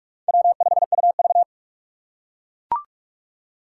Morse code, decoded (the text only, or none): W5UV